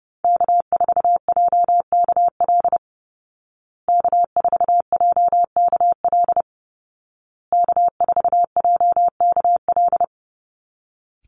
Morse code, heard as a keyed tone, about 30 wpm, 700 Hz. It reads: K4JKL K4JKL K4JKL
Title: Call Signs Repeated 2x - 30wpm